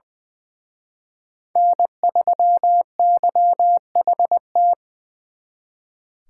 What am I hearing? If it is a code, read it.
Morse code, heard as a keyed tone, about 24 wpm, 700 Hz. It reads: N3YHT
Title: Speed Racing - USA 1x3 Call Signs 20wpm